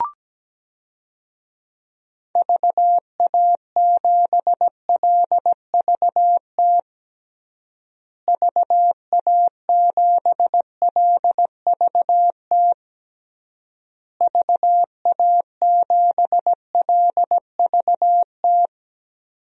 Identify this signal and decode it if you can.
Morse code, heard as a keyed tone, about 17 wpm, 700 Hz. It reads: VA7LVT VA7LVT VA7LVT
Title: DXCC-100 Call Signs - 6 Characters - Repeated 3x - 17wpm